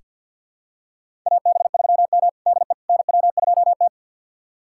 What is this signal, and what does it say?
Morse code, heard as a keyed tone, about 46 wpm, 700 Hz. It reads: AB3M DE NW2T
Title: USA Callsign DE Callsign - 4 Characters - 50wpm